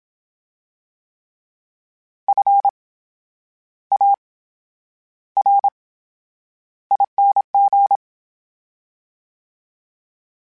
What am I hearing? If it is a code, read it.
Morse code, heard as a keyed tone, about 27 wpm, spaced as for 9 wpm, 800 Hz. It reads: FARING
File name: Special INGs / Special INGs-mp3s / VerySlowToFaster Ing @25